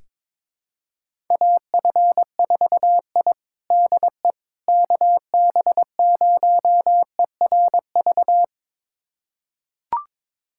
Morse code, heard as a keyed tone, about 22 wpm, 700 Hz. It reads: AF4I DE KB0ERV
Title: USA Callsign DE Callsign 22wpm